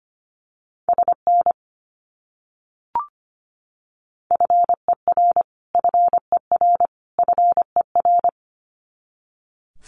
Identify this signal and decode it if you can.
Morse code, heard as a keyed tone, about 25 wpm, 700 Hz. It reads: SN FER FER FER
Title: Abbreviations Repeated 3x - 25wpm